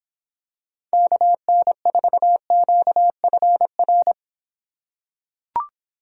Morse code, heard as a keyed tone, about 26 wpm, 700 Hz. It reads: KN4QFR